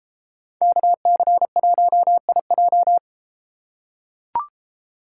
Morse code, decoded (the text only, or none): KC1IJ